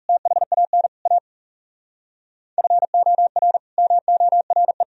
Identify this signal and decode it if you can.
Morse code, heard as a keyed tone, about 37 wpm, 700 Hz. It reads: THAN A FOR MORE
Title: Speed-Racing - Sets of 4 Words 30wpm